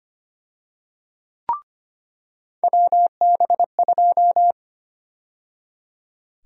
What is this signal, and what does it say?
Morse code, heard as a keyed tone, about 25 wpm, 700 Hz. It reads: WB2
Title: USA Call Sign Prefixes - 25wpm